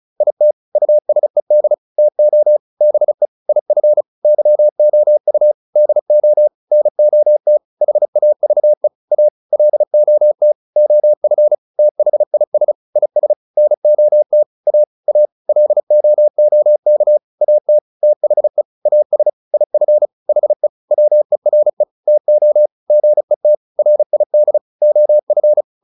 Morse code, heard as a keyed tone, about 34 wpm, 600 Hz. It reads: IT USED TO BE IF YOU DO NOT HAVE A LOT OF THIS IS NOT A A LOOK AT THE AS IF HE WERE TO GET RID OF